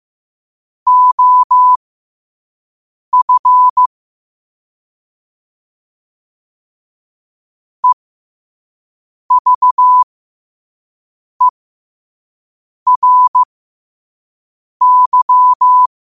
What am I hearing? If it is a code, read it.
Morse code, heard as a keyed tone, about 15 wpm, spaced as for 5 wpm, 1000 Hz. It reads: OF EVERY
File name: 5wpm